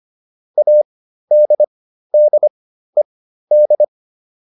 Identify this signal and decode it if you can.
Morse code, heard as a keyed tone, about 25 wpm, spaced as for 13 wpm, 600 Hz. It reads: ADDED